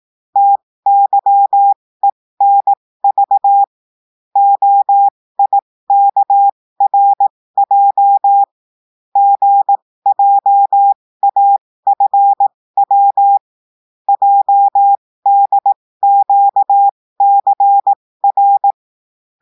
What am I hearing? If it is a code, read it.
Morse code, heard as a keyed tone, about 18 wpm, spaced as for 15 wpm, 800 Hz. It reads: TYENV OIKRJ GJAFW JDQCR
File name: cours-du-lundi files / cours-1580-texte-chiffre-15-mpm